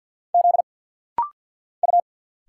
Morse code, heard as a keyed tone, about 50 wpm, 700 Hz. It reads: Z U
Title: Single Letters - Rapid-Fire 50wpm